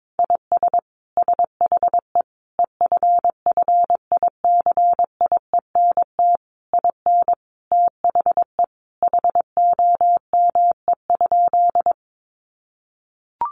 Morse code, heard as a keyed tone, about 22 wpm, 700 Hz. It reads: IS SHE EFFICIENT IN THE HOME?
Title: Sentences from Top 1300 Words 22wpm